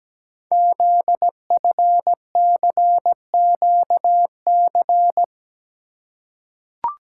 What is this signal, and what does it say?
Morse code, heard as a keyed tone, about 17 wpm, 700 Hz. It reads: ZFCQC
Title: Code Groups - 5 Letters - 17wpm